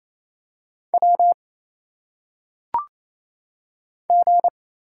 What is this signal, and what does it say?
Morse code, heard as a keyed tone, about 27 wpm, 700 Hz. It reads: W G
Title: Single Letters - Phonetic Pronunciation 28wpm